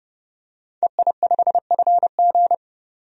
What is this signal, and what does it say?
Morse code, heard as a keyed tone, about 30 wpm, 700 Hz. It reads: EI5FG
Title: DXCC-100 Call Signs - 5 Characters - 30wpm